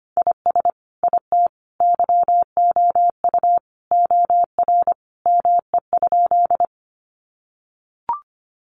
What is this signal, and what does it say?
Morse code, heard as a keyed tone, about 25 wpm, 700 Hz. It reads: IS IT YOU OR ME?
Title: Sentences from Top 200 Words 25wpm